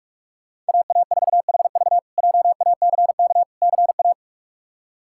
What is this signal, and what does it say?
Morse code, heard as a keyed tone, about 47 wpm, 700 Hz. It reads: AA4HV JACK CA
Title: K1USN SST QSO Practice - 45wpm